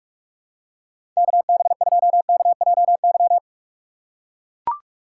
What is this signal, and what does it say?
Morse code, heard as a keyed tone, about 44 wpm, 700 Hz. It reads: KB2KJY